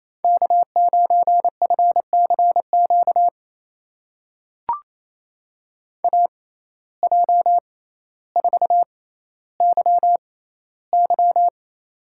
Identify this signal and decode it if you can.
Morse code, heard as a keyed tone, about 28 wpm, spaced as for 14 wpm, 700 Hz. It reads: K9FCQ AJ4YY
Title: USA Call Signs - 5 Characters - 6x Character Spacing - 28wpm